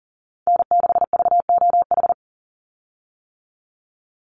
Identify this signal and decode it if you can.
Morse code, heard as a keyed tone, about 38 wpm, 700 Hz. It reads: N6VOH
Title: Call Signs - 40wpm